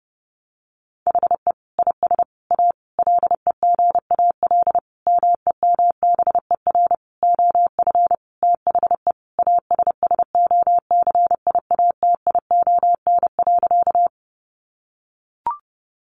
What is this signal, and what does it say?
Morse code, heard as a keyed tone, about 30 wpm, 700 Hz. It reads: HE IS A LEGAL MEMBER OF THE ASSOCIATION.
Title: Sentences from Top 1200 Words 30wpm